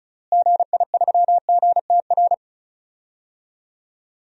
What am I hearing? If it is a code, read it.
Morse code, heard as a keyed tone, about 36 wpm, 700 Hz. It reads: GI3GTR
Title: DXCC-100 Call Signs - 6 Characters - Repeated 2x - 35wpm